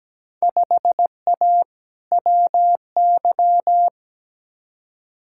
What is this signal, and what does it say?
Morse code, heard as a keyed tone, about 17 wpm, 700 Hz. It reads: 5A WY
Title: Field Day Exchanges Encore - 17wpm